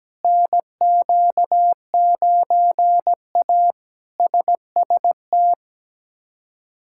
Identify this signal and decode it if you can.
Morse code, heard as a keyed tone, about 17 wpm, 700 Hz. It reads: NQ9A SST
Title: K1USN SST QSO Practice - 17wpm